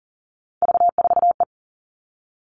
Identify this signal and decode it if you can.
Morse code, heard as a keyed tone, about 40 wpm, 700 Hz. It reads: V4E
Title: Three Challenging Letter-Number Combinations 40wpm